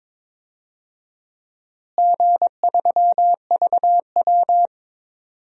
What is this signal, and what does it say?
Morse code, heard as a keyed tone, about 21 wpm, 700 Hz. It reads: G3VW
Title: DXCC-100 Call Signs - 4 Characters - Repeated 3x - 22wpm